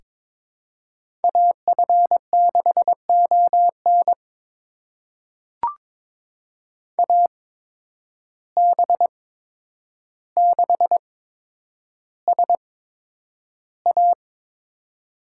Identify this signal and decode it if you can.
Morse code, heard as a keyed tone, about 22 wpm, spaced as for 9 wpm, 700 Hz. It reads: AF6ON AB6SA
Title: USA Call Signs - 5 Characters - 8x Character Spacing - 22wpm